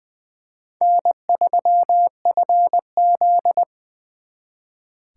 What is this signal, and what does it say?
Morse code, heard as a keyed tone, about 23 wpm, 700 Hz. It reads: N3FZ
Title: Speed Racing - USA 1x2 Call Signs 20wpm